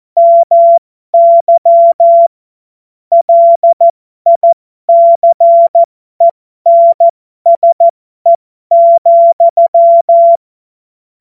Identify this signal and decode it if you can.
Morse code, heard as a keyed tone, about 14 wpm, spaced as for 12 wpm, 680 Hz. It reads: MY LICENSE,
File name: Morsekurs / Witze